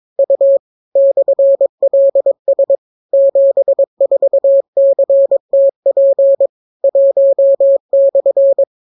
Morse code, heard as a keyed tone, about 22 wpm, 550 Hz. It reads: U /LS 74CTP 1/